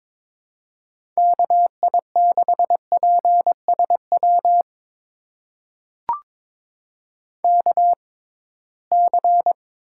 Morse code, heard as a keyed tone, about 22 wpm, spaced as for 17 wpm, 700 Hz. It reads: KI6PSW KC
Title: USA Call Signs - 6 Characters - 6x Character Spacing - 22wpm